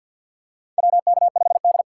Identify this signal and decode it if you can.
Morse code, heard as a keyed tone, about 53 wpm, 700 Hz. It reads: WK5D
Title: USA Call Signs - 4x Character Spacing - 50wpm